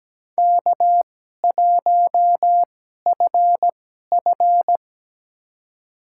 Morse code, heard as a keyed tone, about 17 wpm, spaced as for 12 wpm, 700 Hz. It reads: K1FF
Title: USA Call Signs - 4 Characters - 2x Character Spacing - 17wpm